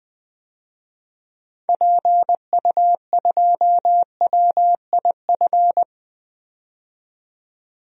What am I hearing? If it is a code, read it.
Morse code, heard as a keyed tone, about 20 wpm, 700 Hz. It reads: PU2WIF
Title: DXCC-100 Call Signs - 6 Characters - 20wpm